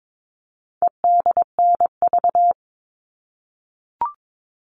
Morse code, heard as a keyed tone, about 22 wpm, 700 Hz. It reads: EDNV
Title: All Cut Numbers - 4-Digit 22wpm